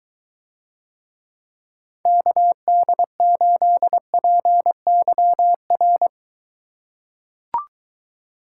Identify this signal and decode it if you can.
Morse code, heard as a keyed tone, about 22 wpm, 700 Hz. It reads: KD8PYR